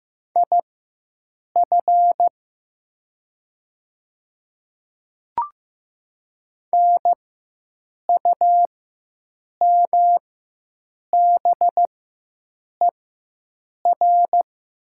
Morse code, heard as a keyed tone, about 15 wpm, spaced as for 7 wpm, 700 Hz. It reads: IF NUMBER
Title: 4x Character Spacing - Top 200 Words - 15wpm